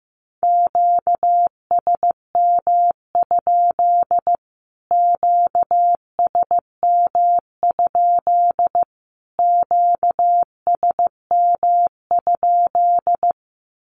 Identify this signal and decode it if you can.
Morse code, heard as a keyed tone, about 15 wpm, 700 Hz. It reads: QSM? QSM? QSM?
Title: Q-Codes Repeated 3x 15wpm